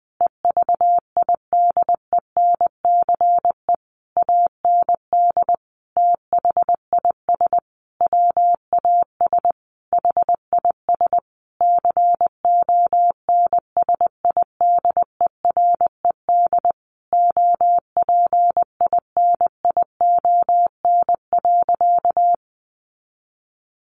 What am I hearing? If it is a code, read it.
Morse code, heard as a keyed tone, about 20 wpm, 700 Hz. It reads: EVIDENCE AND THIS WAS HIS CONSIDERED OPINION.
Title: A Sound Of Thunder 20wpm